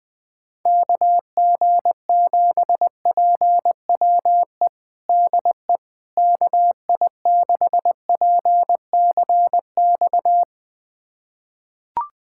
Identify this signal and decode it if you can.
Morse code, heard as a keyed tone, about 20 wpm, 700 Hz. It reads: KG7PWE DE KI6PCX
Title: USA Callsign DE Callsign - 6 Characters - 20wpm